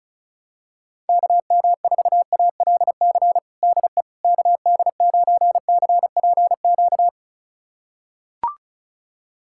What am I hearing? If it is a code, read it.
Morse code, heard as a keyed tone, about 36 wpm, 700 Hz. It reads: KM4ALC DE KD9CPQ
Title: USA Callsign DE Callsign 35wpm